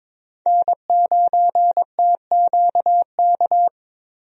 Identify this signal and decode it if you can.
Morse code, heard as a keyed tone, about 22 wpm, 700 Hz. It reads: N9TQK
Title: USA Call Signs - 5 Characters - Repeated 3x - 22wpm